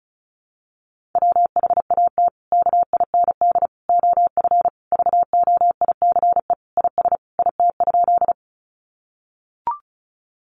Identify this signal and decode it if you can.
Morse code, heard as a keyed tone, about 34 wpm, 700 Hz. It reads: WHAT KIND OF VOICE IS IT?
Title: Sentences from Top 500 Words 35wpm